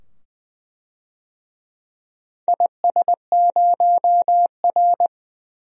Morse code, heard as a keyed tone, about 20 wpm, 700 Hz. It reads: IS0R
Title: DXCC-100 Call Signs - 4 Characters - 20wpm